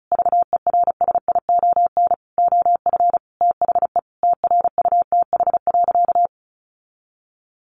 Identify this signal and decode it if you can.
Morse code, heard as a keyed tone, about 34 wpm, 700 Hz. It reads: VERSION OF THE TRUTH.
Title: Sentences from Top 1400 Words 35wpm